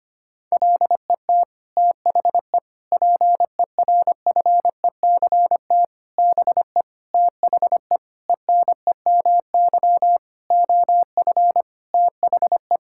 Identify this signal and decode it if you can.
Morse code, heard as a keyed tone, about 25 wpm, 700 Hz. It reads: LET THE PERFECT BE THE ENEMY OF THE
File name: Sets of 9 Words - 25wpm